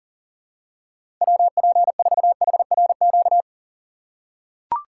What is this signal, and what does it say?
Morse code, heard as a keyed tone, about 40 wpm, 700 Hz. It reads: WP4HRQ
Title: USA Call Signs - 6 Characters - 3x Character Spacing - 40wpm